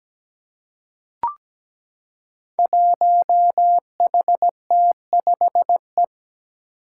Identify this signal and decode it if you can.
Morse code, heard as a keyed tone, about 17 wpm, 700 Hz. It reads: 1HT5E
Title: Five Challenging Letter-Number Combinations 17wpm